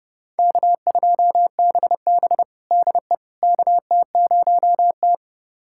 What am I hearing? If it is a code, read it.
Morse code, heard as a keyed tone, about 30 wpm, 700 Hz. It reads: K2BB DE KT0T
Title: USA Callsign DE Callsign - 4 Characters - 30wpm